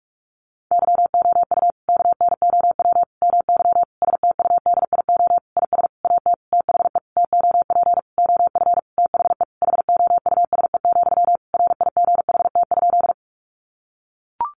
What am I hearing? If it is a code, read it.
Morse code, heard as a keyed tone, about 44 wpm, 700 Hz. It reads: YOU KNOW MY STUDIO IS AT THE TOP OF THE HOUSE, RIGHT?
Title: Sentences from Top 1200 Words 45wpm